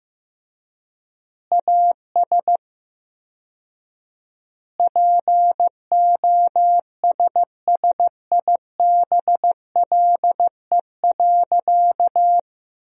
Morse code, heard as a keyed tone, about 15 wpm, 700 Hz. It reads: AS POSSIBLE.
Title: 4x Word-Spacing - Sentences from Top 500 Words - 15wpm